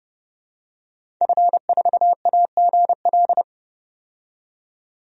Morse code, heard as a keyed tone, about 30 wpm, 700 Hz. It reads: F4AGL
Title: DXCC-100 Call Signs - 5 Characters - 30wpm